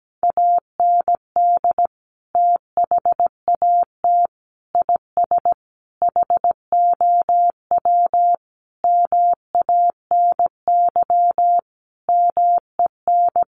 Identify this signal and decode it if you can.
Morse code, heard as a keyed tone, about 17 wpm, 700 Hz. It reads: AND THAT IS HOW MANY MEN
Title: Sentences from Top 1400 Words 17wpm